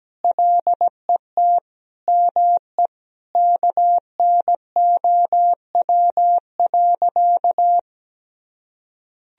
Morse code, heard as a keyed tone, about 17 wpm, 700 Hz. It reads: LET ME KNOW.
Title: Sentences from Top 1000 Words - Review Part 2 - 17wpm